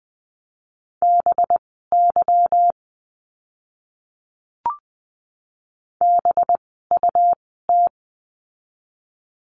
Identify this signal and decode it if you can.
Morse code, heard as a keyed tone, about 20 wpm, spaced as for 15 wpm, 700 Hz. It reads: BY BUT
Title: 237 Lesson 38f - Letter q - Words 20wpm